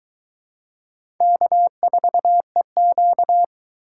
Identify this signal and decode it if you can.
Morse code, heard as a keyed tone, about 23 wpm, 700 Hz. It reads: K4EQ